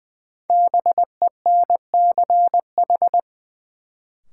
Morse code, heard as a keyed tone, about 20 wpm, 700 Hz. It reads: BENCH
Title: Sentences from Top 1400 Words - New Word Drill Alt - 20wpm